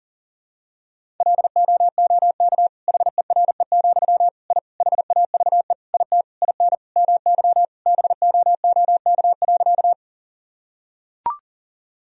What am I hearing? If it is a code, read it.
Morse code, heard as a keyed tone, about 40 wpm, 700 Hz. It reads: LOOK HERE, I HAVE IT IN MY BOOK.
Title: Sentences from Top 1000 Words - Review Part 1 - 40wpm